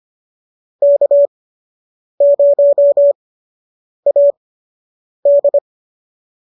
Morse code, heard as a keyed tone, about 25 wpm, spaced as for 8 wpm, 590 Hz. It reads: K0AD